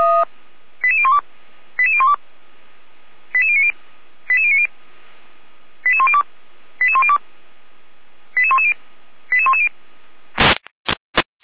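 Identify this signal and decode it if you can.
Five-tone selective call (ZVEI), heard as quick runs of five tones, tones of about 70 ms. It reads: 88012 88012 88090 88090 88102 88102 88110 88110
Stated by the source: ZVEI five-tone call 88451 received on an SDR.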